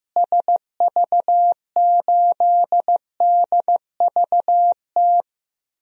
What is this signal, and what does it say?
Morse code, heard as a keyed tone, about 15 wpm, 700 Hz. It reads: SV8DVT
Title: DXCC-100 Call Signs - 6 Characters - Repeated 3x - 15wpm